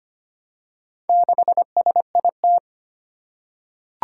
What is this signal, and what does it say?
Morse code, heard as a keyed tone, about 25 wpm, 700 Hz. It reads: 6SIT
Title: Four Challenging Letter-Number Combinations 25wpm